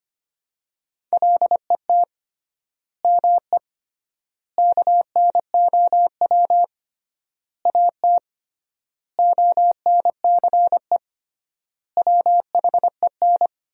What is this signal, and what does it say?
Morse code, heard as a keyed tone, about 25 wpm, 700 Hz. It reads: LET ME KNOW AT ONCE WHEN
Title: 3x Word-Spacing - Sentences from Top 300 Words - 25wpm